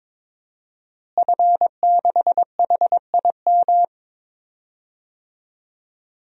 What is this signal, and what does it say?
Morse code, heard as a keyed tone, about 21 wpm, 700 Hz. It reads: F6HIM
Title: DXCC-100 Call Signs - 5 Characters - Repeated 2x - 22wpm